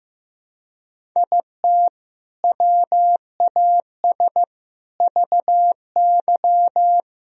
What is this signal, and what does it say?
Morse code, heard as a keyed tone, about 15 wpm, 700 Hz. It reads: IT WAS VY
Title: Sentences from Top 500 Words and Abbreviations 15wpm